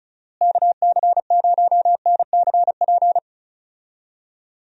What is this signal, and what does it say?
Morse code, heard as a keyed tone, about 34 wpm, 700 Hz. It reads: KC0NCP
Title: USA Call Signs - 6 Characters - Repeated 3x - 35wpm